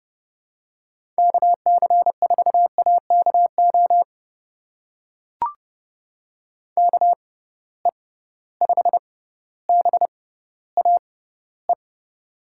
Morse code, heard as a keyed tone, about 30 wpm, spaced as for 15 wpm, 700 Hz. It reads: KC4AKO KE5BAE
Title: USA Call Signs - 6 Characters - 6x Character Spacing - 30wpm